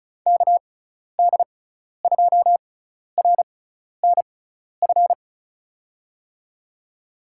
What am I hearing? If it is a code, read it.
Morse code, heard as a keyed tone, about 36 wpm, spaced as for 12 wpm, 700 Hz. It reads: KD2RNF
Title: USA Call Signs - 6 Characters - 6x Character Spacing - 35wpm